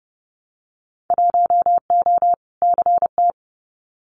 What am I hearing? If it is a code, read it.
Morse code, heard as a keyed tone, about 30 wpm, 700 Hz. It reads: 1O CT
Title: Winter Field Day Exchanges Encore 30wpm